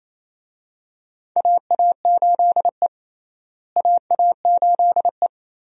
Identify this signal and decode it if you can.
Morse code, heard as a keyed tone, about 28 wpm, 700 Hz. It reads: AA8E AA8E
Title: USA Call Signs - 4 Characters - Repeated 2x - 28wpm